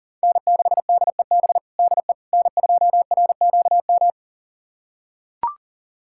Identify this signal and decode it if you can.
Morse code, heard as a keyed tone, about 40 wpm, 700 Hz. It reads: N6DEB DE N2RQM
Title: USA Callsign DE Callsign - 5 Characters - 40wpm